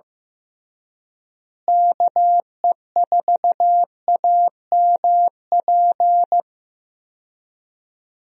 Morse code, heard as a keyed tone, about 15 wpm, 700 Hz. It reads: KE4AMP
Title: USA Call Signs - 6 Characters - Repeated 2x - 15wpm